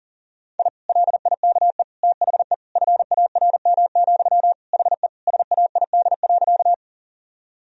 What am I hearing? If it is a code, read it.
Morse code, heard as a keyed tone, about 40 wpm, 700 Hz. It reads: I LIKE THE FARM, HE SAID.
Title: Sentences from Top 1000 Words - Review Part 2 - 40wpm